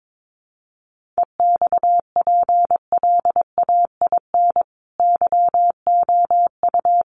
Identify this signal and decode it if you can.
Morse code, heard as a keyed tone, about 22 wpm, 700 Hz. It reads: EXPLAIN YOU
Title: Sentences from Top 1100 Words 22wpm